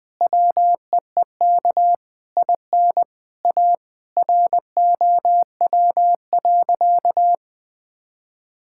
Morse code, heard as a keyed tone, about 20 wpm, 700 Hz. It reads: WEEK IN A ROW.
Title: Sentences with 7-Word Phrases - Spoken Beginning - 20wpm